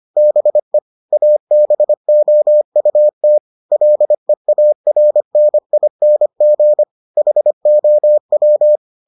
Morse code, heard as a keyed tone, about 25 wpm, 600 Hz. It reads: BE ABOUT LEARNING HOW